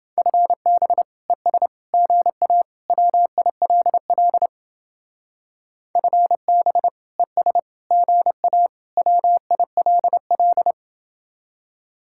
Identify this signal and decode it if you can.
Morse code, heard as a keyed tone, about 30 wpm, 700 Hz. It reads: FB ES GA WILL FB ES GA WILL